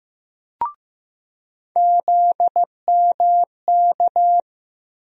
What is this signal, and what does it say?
Morse code, heard as a keyed tone, about 15 wpm, 700 Hz. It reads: ZMK
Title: USA Call Sign Postfixes - 3 Letters 15wpm